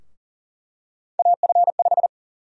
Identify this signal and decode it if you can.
Morse code, heard as a keyed tone, about 40 wpm, 700 Hz. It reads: AF5
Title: USA Call Sign Prefixes - Repeated 2x - 40wpm